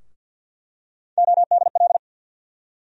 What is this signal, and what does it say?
Morse code, heard as a keyed tone, about 51 wpm, 700 Hz. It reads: ODL
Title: USA Call Sign Postfixes - 3 Letters 50wpm